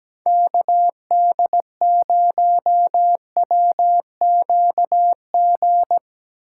Morse code, heard as a keyed tone, about 17 wpm, 700 Hz. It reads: KD0WQG
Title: USA Call Signs - 6x Character Spacing - 17wpm